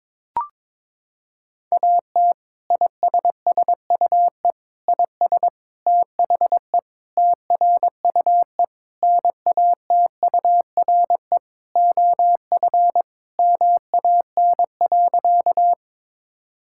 Morse code, heard as a keyed tone, about 22 wpm, 700 Hz. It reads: AT ISSUE IS THE TRUE NATURE OF MAN.
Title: Sentences from Top 1000 Words - Review Part 6 - 22wpm